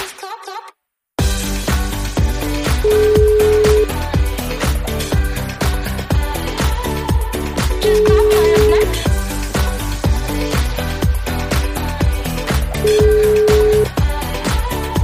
Ringback tone (Europe, 425 Hz), with noise.